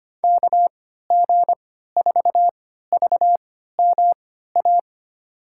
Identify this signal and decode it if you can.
Morse code, heard as a keyed tone, about 25 wpm, spaced as for 14 wpm, 700 Hz. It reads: KG4VMA
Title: USA Call Signs - 6 Characters - 3x Character Spacing - 25wpm